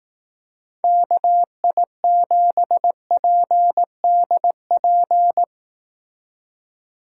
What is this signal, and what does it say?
Morse code, heard as a keyed tone, about 18 wpm, 700 Hz. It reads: KI7PDP
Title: Speed Racing - USA 2x3 Call Signs 15wpm